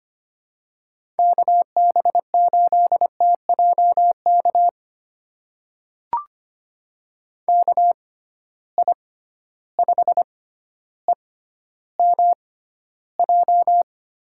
Morse code, heard as a keyed tone, about 25 wpm, spaced as for 13 wpm, 700 Hz. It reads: KB8TJK KI5EMJ
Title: USA Call Signs - 6 Characters - 6x Character Spacing - 25wpm